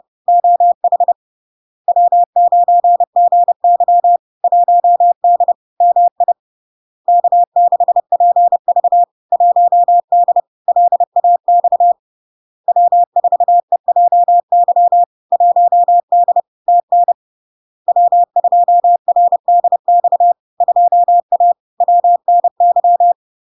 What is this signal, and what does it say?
Morse code, heard as a keyed tone, about 30 wpm, 700 Hz. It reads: OH W9GY 1D MI K6PV 1D LAX W4EJY 1D TN W2RDX 2A WNY
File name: LARK-FD-2022-Log-in-CW-30wpm-700-Hz